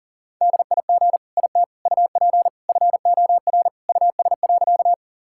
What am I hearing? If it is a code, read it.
Morse code, heard as a keyed tone, about 40 wpm, 700 Hz. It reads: DIG IT UP FOR US.
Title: Sentences from Top 1600 Words 40wpm